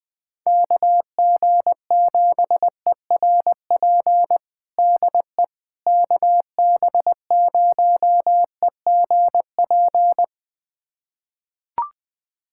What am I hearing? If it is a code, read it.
Morse code, heard as a keyed tone, about 20 wpm, 700 Hz. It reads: KG7ERP DE KB0EGP
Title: USA Callsign DE Callsign 20wpm